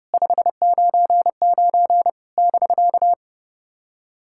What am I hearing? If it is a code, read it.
Morse code, heard as a keyed tone, about 30 wpm, 700 Hz. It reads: 599 <BK>
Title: SOTA QSO Practice 30wpm